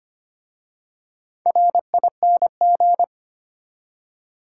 Morse code, heard as a keyed tone, about 25 wpm, 700 Hz. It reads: RING
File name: practice / 25wpm / Easy Words with Hints - 25wpm